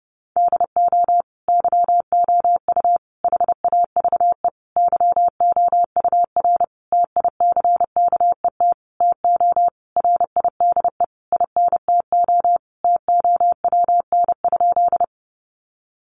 Morse code, heard as a keyed tone, about 30 wpm, 700 Hz. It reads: DO YOU HAVE YOUR TICKET TO RIDE INTO TOWN?
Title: Sentences from Top 1400 Words 30wpm